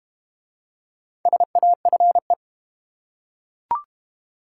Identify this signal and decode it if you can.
Morse code, heard as a keyed tone, about 30 wpm, 690 Hz. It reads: SAFE
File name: Easy Words - Speed-Racing with Hints - 25wpm